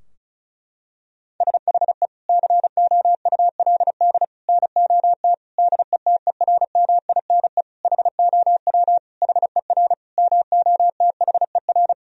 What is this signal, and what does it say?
Morse code, heard as a keyed tone, about 34 wpm, 700 Hz. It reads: SHE COULD NOT DETERMINE HOW HER MOTHER
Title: Sentences from Top 1000 Words - Review Part 4 - 35wpm